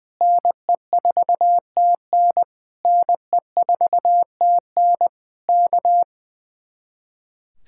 Morse code, heard as a keyed tone, about 19 wpm, 700 Hz. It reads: NE4TN NE4TN K
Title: Speed-Racing Common QSO Phrases Encore - 20wpm